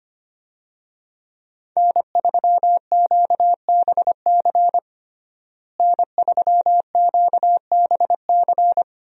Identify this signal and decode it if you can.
Morse code, heard as a keyed tone, about 25 wpm, 700 Hz. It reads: N3QBC N3QBC
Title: USA Call Signs - 5 Characters - Repeated 2x - 25wpm